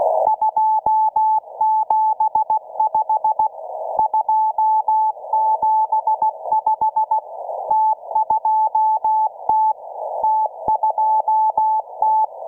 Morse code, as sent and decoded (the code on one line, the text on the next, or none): ..--- --... ..... / ..--- --... ..... / - ..--- - / - ..--- -
275 275 T2T T2T